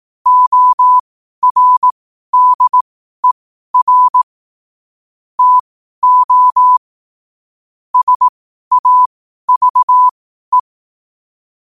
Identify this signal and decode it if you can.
Morse code, heard as a keyed tone, about 18 wpm, spaced as for 12 wpm, 1000 Hz. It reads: ORDER TO SAVE
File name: 2024-09-14-12wpm